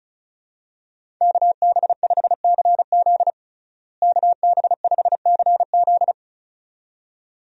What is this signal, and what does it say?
Morse code, heard as a keyed tone, about 36 wpm, 700 Hz. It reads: KB5CZ KB5CZ
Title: USA Call Signs - 5 Characters - Repeated 2x - 35wpm